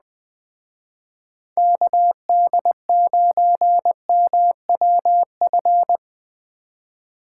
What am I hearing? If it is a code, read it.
Morse code, heard as a keyed tone, about 20 wpm, 700 Hz. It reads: KD9MWF